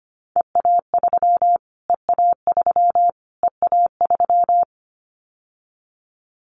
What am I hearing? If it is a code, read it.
Morse code, heard as a keyed tone, about 25 wpm, 700 Hz. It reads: EA3 EA3 EA3
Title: Callsign Prefixes for Top 25 DXCC - Repeated 3x - 25wpm